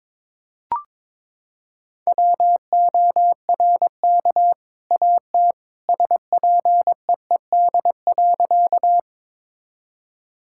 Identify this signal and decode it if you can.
Morse code, heard as a keyed tone, about 22 wpm, 700 Hz. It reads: WORK AT SPEED.
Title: Sentences from Top 1000 Words - Review Part 5 - 22wpm